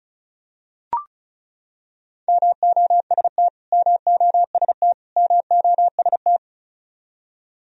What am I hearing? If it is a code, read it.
Morse code, heard as a keyed tone, about 34 wpm, 700 Hz. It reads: MOST MOST MOST
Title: Top 100 Words Repeated 35wpm